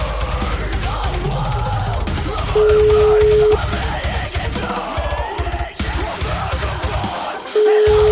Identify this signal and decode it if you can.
Ringback tone (Europe, 425 Hz), with noise.
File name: zRDtLZUuQAJFwIA